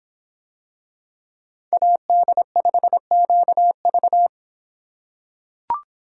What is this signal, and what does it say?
Morse code, heard as a keyed tone, about 24 wpm, 700 Hz. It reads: AD5QV